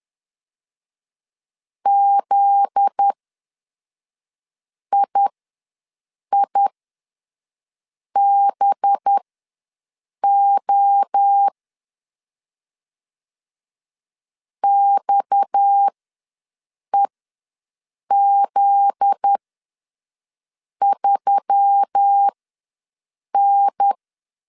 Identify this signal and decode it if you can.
Morse code, heard as a keyed tone, about 10 wpm, spaced as for 5 wpm, 800 Hz. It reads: ZIIBO XEZ3N
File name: LEZIONI / 7 GF9Z / 7 GF9Z ws 10wpm 3919